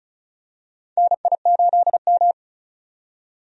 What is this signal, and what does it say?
Morse code, heard as a keyed tone, about 34 wpm, 700 Hz. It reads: NI8M
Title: USA Call Signs - 4 Characters - 35wpm